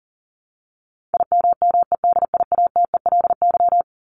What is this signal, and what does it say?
Morse code, heard as a keyed tone, about 40 wpm, 700 Hz. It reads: IMMEDIATELY
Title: LY Suffix - Top 100 Words with Base Words 40wpm